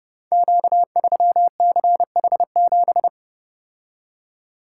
Morse code, heard as a keyed tone, about 30 wpm, 700 Hz. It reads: Q3CH7
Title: Code Groups - 5 Letter:Numbers - 30wpm